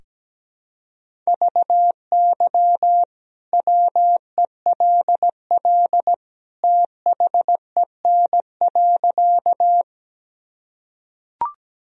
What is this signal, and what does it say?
Morse code, heard as a keyed tone, about 17 wpm, 700 Hz. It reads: VY WELL THEN.
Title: Sentences from Top 200 Words and Abbreviations 17wpm